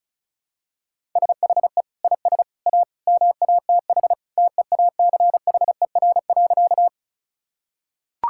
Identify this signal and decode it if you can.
Morse code, heard as a keyed tone, about 36 wpm, 700 Hz. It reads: SHE IS A MATH TEACHER.
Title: Sentences from Top 1000 Words - Review Part 6 - 35wpm